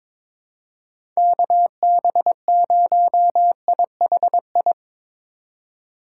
Morse code, heard as a keyed tone, about 23 wpm, 700 Hz. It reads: KB0IHI
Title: USA Call Signs - 6 Characters - 3x Character Spacing - 22wpm